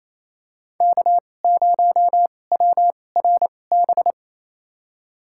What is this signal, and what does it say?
Morse code, heard as a keyed tone, about 28 wpm, spaced as for 20 wpm, 700 Hz. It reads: K0WRB
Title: USA Call Signs - 2x Character Spacing - 28wpm